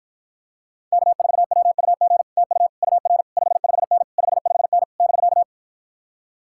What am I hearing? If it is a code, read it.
Morse code, heard as a keyed tone, about 52 wpm, 700 Hz. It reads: K4WUG TU UR 55N 55N <BK>